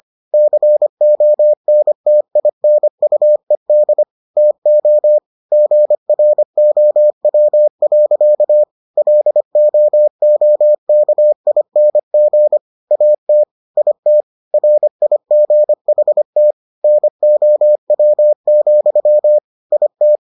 Morse code, heard as a keyed tone, about 25 wpm, 600 Hz. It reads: CONTINUED TO GROW. LOOKING AT IT RIGHT NOW, IT